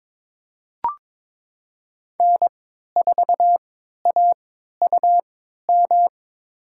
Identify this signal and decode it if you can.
Morse code, heard as a keyed tone, about 22 wpm, spaced as for 12 wpm, 700 Hz. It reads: N4AUM
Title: USA Call Signs - 5 Characters - 3x Character Spacing - 22wpm